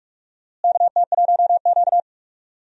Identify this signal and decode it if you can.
Morse code, heard as a keyed tone, about 46 wpm, 700 Hz. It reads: KT1Q
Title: USA Call Signs - 4 Characters - Repeated 3x - 45wpm